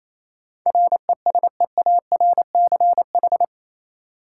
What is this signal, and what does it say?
Morse code, heard as a keyed tone, about 28 wpm, 700 Hz. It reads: RESEARCH
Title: Sentences from Top 1000 Words - Review Part 4 - New Word Drill - 28wpm